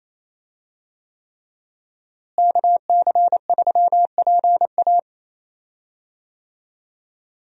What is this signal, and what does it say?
Morse code, heard as a keyed tone, about 28 wpm, 700 Hz. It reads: KC3PA
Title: USA Call Signs - 5 Characters - Repeated 3x - 28wpm